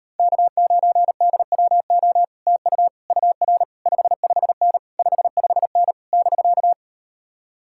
Morse code, heard as a keyed tone, about 39 wpm, 700 Hz. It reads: K9DWO TU UR 55N 55N <BK>